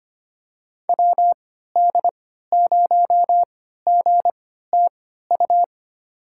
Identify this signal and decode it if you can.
Morse code, heard as a keyed tone, about 25 wpm, spaced as for 14 wpm, 700 Hz. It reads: WD0GTU
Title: USA Call Signs - 3x Character Spacing - 25wpm